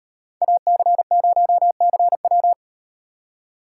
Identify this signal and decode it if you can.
Morse code, heard as a keyed tone, about 38 wpm, 700 Hz. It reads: AC0CW